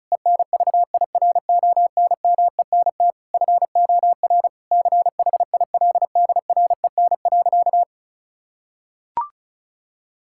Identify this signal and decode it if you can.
Morse code, heard as a keyed tone, about 36 wpm, 700 Hz. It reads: ENVIRONMENT FOR CHILDREN.
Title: Sentences from Top 1000 Words - Review Part 5 - 35wpm